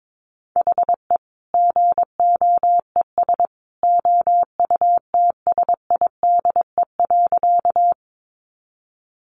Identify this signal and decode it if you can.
Morse code, heard as a keyed tone, about 22 wpm, 700 Hz. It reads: HE GOES OUTSIDE.
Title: Additional Easy Phrases 22wpm